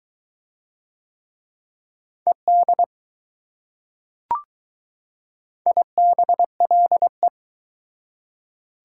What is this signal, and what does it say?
Morse code, heard as a keyed tone, about 22 wpm, 700 Hz. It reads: ED IBLE